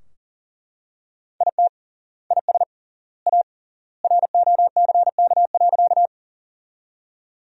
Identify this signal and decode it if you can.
Morse code, heard as a keyed tone, about 40 wpm, 700 Hz. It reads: IT IS A ROCK.
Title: 3x Word-Spacing - Sentences from Top 400 Words - 40wpm